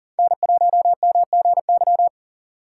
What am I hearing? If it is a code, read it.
Morse code, heard as a keyed tone, about 40 wpm, 700 Hz. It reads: N1MGY
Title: USA Call Signs - 4x Character Spacing - 40wpm